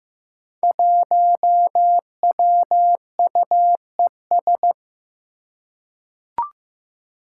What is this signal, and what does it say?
Morse code, heard as a keyed tone, about 15 wpm, 700 Hz. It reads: 1WUES
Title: Five Challenging Letter-Number Combinations 15wpm